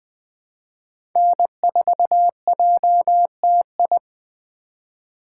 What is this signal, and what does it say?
Morse code, heard as a keyed tone, about 19 wpm, 700 Hz. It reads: N4JTI